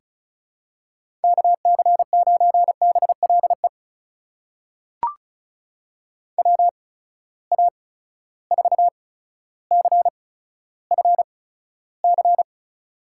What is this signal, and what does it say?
Morse code, heard as a keyed tone, about 36 wpm, spaced as for 15 wpm, 700 Hz. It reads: KC9BLE WA4CFC
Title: USA Call Signs - 6 Characters - 8x Character Spacing - 35wpm